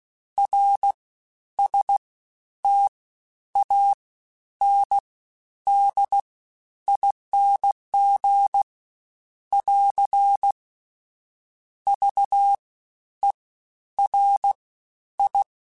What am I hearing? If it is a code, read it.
Morse code, heard as a keyed tone, about 16 wpm, 800 Hz. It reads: R S T A N D ING <AR> V E R I
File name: Special INGs-mp3s / Special INGs-mp3s / Fast ing @15